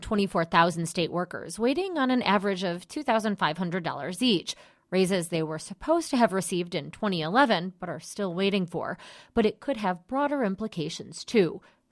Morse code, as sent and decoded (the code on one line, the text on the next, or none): none